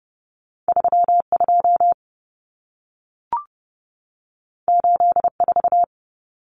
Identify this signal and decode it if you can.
Morse code, heard as a keyed tone, about 30 wpm, 700 Hz. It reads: 32 84
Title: Numbers - 2-Digit 30wpm